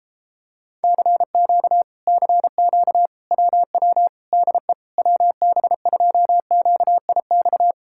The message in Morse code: CQ CQ WW DE WB2QIX

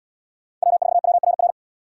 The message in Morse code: W4WRR

